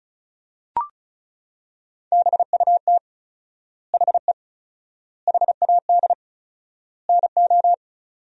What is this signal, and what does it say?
Morse code, heard as a keyed tone, about 36 wpm, 700 Hz. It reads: BUT HE HAD NO
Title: 4x Word-Spacing - Sentences from Top 100 Words - 35wpm